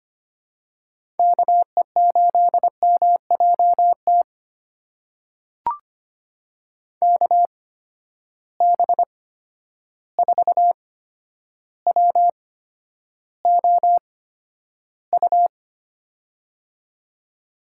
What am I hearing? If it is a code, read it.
Morse code, heard as a keyed tone, about 25 wpm, spaced as for 11 wpm, 700 Hz. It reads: KE8MJT KB4WOU
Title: USA Call Signs - 6 Characters - 8x Character Spacing - 25wpm